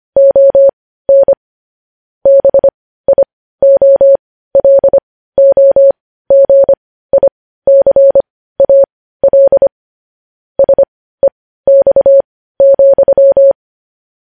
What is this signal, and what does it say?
Morse code, heard as a keyed tone, about 25 wpm, spaced as for 15 wpm, 550 Hz. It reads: ON BIOLOGICAL SEX,
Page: News headlines in morse code at 15 wpm